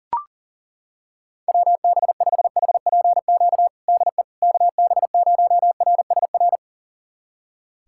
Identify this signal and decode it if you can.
Morse code, heard as a keyed tone, about 40 wpm, 700 Hz. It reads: WB5HPQ DE KB0RSR
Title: USA Callsign DE Callsign - 6 Characters - 40wpm